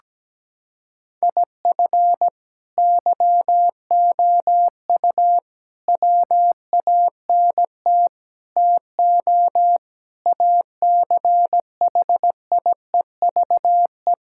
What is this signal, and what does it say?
Morse code, heard as a keyed tone, about 17 wpm, 700 Hz. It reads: IF YOU WANT TO ACHIEVE
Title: Sentences from Top 1000 Words - Review Part 6 - 17wpm